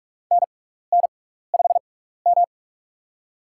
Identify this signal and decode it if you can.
Morse code, heard as a keyed tone, about 44 wpm, spaced as for 15 wpm, 700 Hz. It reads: NN5M
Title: USA Call Signs - 6x Character Spacing - 45wpm